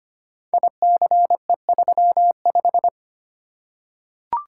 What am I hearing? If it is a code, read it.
Morse code, heard as a keyed tone, about 25 wpm, 700 Hz. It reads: ICE35
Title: Code Groups - 5 Letter:Numbers - 25wpm